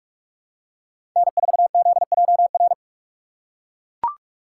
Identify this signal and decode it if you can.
Morse code, heard as a keyed tone, about 40 wpm, 700 Hz. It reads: N4ZJR